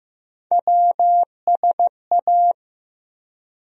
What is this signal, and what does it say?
Morse code, heard as a keyed tone, about 15 wpm, 700 Hz. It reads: WSA
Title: USA Call Sign Postfixes - 3 Letters 15wpm